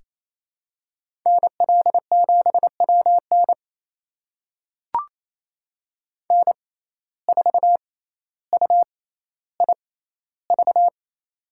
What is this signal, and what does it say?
Morse code, heard as a keyed tone, about 28 wpm, spaced as for 14 wpm, 700 Hz. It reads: NL7WN N4UIV
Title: USA Call Signs - 5 Characters - 6x Character Spacing - 28wpm